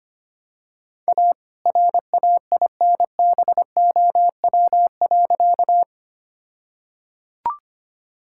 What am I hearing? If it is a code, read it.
Morse code, heard as a keyed tone, about 25 wpm, 700 Hz. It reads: A RAINBOW.
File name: practice / 25wpm / All-Age Appropriate Jokes - 25wpm